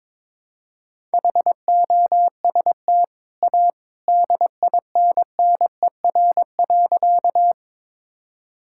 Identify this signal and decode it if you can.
Morse code, heard as a keyed tone, about 22 wpm, 690 Hz. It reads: HOST A DINNER.
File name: practice / 22wpm / Easy Sentences with Hints - 22wpm